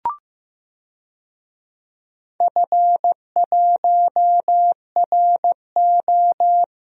Morse code, heard as a keyed tone, about 15 wpm, 700 Hz. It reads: F1RO
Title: DXCC-100 Call Signs - 4 Characters - 15wpm